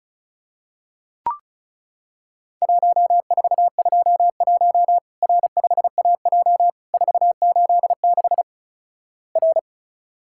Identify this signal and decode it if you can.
Morse code, heard as a keyed tone, about 36 wpm, 700 Hz. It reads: 1421 R5AJ 486 R
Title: Worked All Europe DX Contest - QTC Exchange - 35wpm